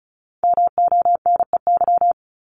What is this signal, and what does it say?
Morse code, heard as a keyed tone, about 36 wpm, 700 Hz. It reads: MONEY
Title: Multi-letter Phonemes Repeated 3x 35wpm